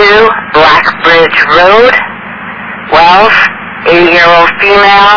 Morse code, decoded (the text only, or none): none